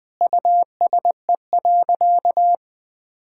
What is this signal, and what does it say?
Morse code, heard as a keyed tone, about 20 wpm, 700 Hz. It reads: USE.
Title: Sentences from Top 1000 Words - Review Part 6 - 20wpm